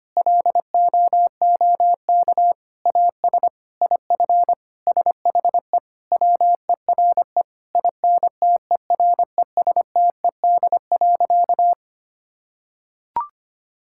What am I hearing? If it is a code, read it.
Morse code, heard as a keyed tone, about 25 wpm, 700 Hz. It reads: LOOK AS IF SHE WERE INTERESTED.
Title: Sentences from Top 1000 Words - Review Part 6 - 25wpm